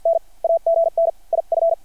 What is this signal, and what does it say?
Morse code, heard as a keyed tone, about 53 wpm, spaced as for 45 wpm, 660 Hz. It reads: N AGN IF
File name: archived / CRSnet